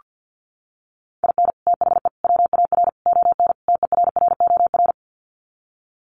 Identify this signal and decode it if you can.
Morse code, heard as a keyed tone, about 48 wpm, 700 Hz. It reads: IN THE WAR ON TERROR
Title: Sets of 5 Words Encore2 50wpm